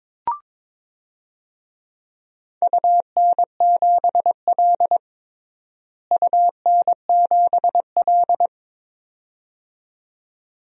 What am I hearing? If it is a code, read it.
Morse code, heard as a keyed tone, about 21 wpm, 700 Hz. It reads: UN7L UN7L
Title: DXCC-100 Call Signs - 4 Characters - Repeated 2x - 22wpm